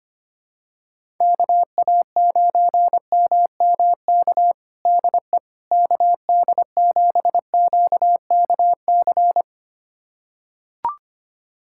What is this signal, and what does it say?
Morse code, heard as a keyed tone, about 25 wpm, 700 Hz. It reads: KA9MMK DE KD7QKC
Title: USA Callsign DE Callsign 25wpm